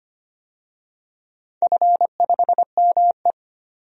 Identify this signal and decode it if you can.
Morse code, heard as a keyed tone, about 25 wpm, 700 Hz. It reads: F5ME
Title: DXCC-100 Call Signs - 4 Characters - 25wpm